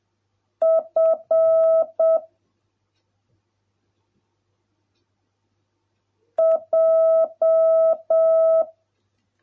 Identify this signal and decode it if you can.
Morse code, heard as a keyed tone, about 7 wpm, 650 Hz. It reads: F J